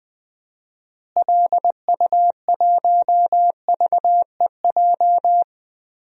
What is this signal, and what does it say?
Morse code, heard as a keyed tone, about 20 wpm, 700 Hz. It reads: LU1VEJ
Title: DXCC-100 Call Signs - 6 Characters - Repeated 3x - 20wpm